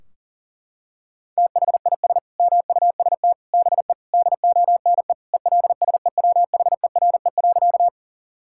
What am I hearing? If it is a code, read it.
Morse code, heard as a keyed tone, about 40 wpm, 700 Hz. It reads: THIS MUST BE DONE ELSEWHERE.
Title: Sentences from Top 1600 Words 40wpm